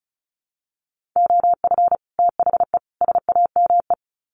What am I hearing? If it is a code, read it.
Morse code, heard as a keyed tone, about 34 wpm, 700 Hz. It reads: OF THE SAME
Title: Sets of 3 Words 35wpm